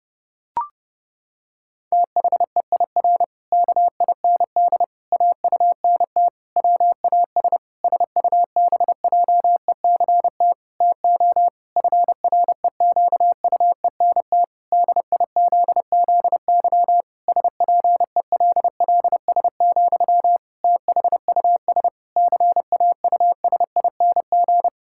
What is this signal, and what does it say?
Morse code, heard as a keyed tone, about 32 wpm, 700 Hz. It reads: THEIR KIND AUNT WAS SUBJECT TO FREQUENT DIZZY SPELLS, THUS CAUSING
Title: Pangram Challenge v2 - 30wpm